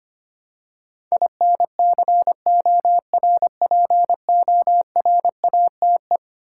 Morse code, heard as a keyed tone, about 25 wpm, 700 Hz. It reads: INCORPORATE
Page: morse-code-ninja-practice-25wpm